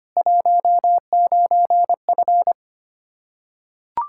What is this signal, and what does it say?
Morse code, heard as a keyed tone, about 25 wpm, 700 Hz. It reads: 19F
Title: Practice QSO - Exchange 2 - Info - 25wpm